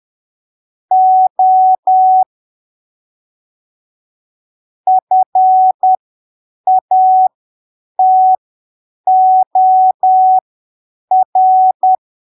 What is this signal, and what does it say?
Morse code, heard as a keyed tone, about 10 wpm, spaced as for 7 wpm, 750 Hz. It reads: O FATOR
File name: arquivos / texto / texto 09 10ppm